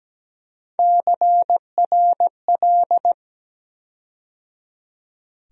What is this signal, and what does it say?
Morse code, heard as a keyed tone, about 17 wpm, 700 Hz. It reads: CRL
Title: USA Call Sign Postfixes - 3 Letters 17wpm